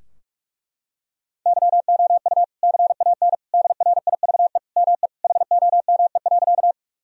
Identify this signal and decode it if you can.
Morse code, heard as a keyed tone, about 41 wpm, 700 Hz. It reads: YOU CAN DRIVE ME HOME.
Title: Sentences from Top 1000 Words - Review Part 2 - 45wpm